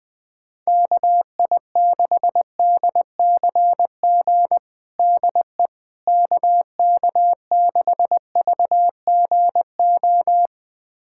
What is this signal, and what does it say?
Morse code, heard as a keyed tone, about 20 wpm, 700 Hz. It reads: KI6DCG DE KK6VGO
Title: USA Callsign DE Callsign - 6 Characters - 20wpm